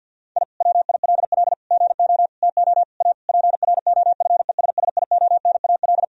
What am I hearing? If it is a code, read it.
Morse code, heard as a keyed tone, about 50 wpm, 700 Hz. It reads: I WILL GO TO A PROFESSIONAL
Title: Sentences from Top 1000 Words - Review Part 4 - 50wpm